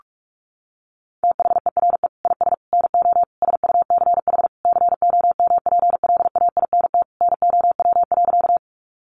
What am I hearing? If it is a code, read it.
Morse code, heard as a keyed tone, about 44 wpm, 700 Hz. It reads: THERE IS NO SUCH COMPLAINT NOW.
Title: Sentences from Top 1400 Words 45wpm